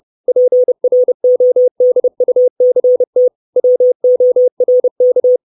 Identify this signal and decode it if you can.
Morse code, heard as a keyed tone, about 30 wpm, 500 Hz. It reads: PRODUCT WORK
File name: 632 Words 30WPM